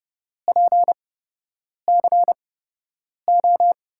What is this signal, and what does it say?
Morse code, heard as a keyed tone, about 30 wpm, spaced as for 8 wpm, 700 Hz. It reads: PCO
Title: USA Call Signs - 6 Characters - 8x Character Spacing - 30wpm